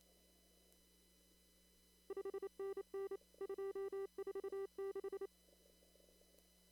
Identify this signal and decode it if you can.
Morse code, heard as a keyed tone, about 28 wpm, 390 Hz. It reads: 5NN 246
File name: CW / kt5j wpx cw-op-k5pi